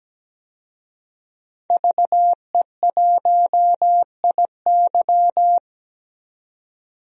Morse code, heard as a keyed tone, about 17 wpm, 700 Hz. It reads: VE1IY
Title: DXCC-100 Call Signs - 5 Characters - 17wpm